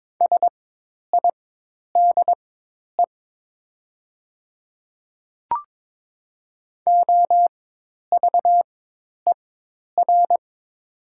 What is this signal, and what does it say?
Morse code, heard as a keyed tone, about 22 wpm, spaced as for 10 wpm, 700 Hz. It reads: SIDE OVER
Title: 4x Character Spacing - Top 200 Words - 22wpm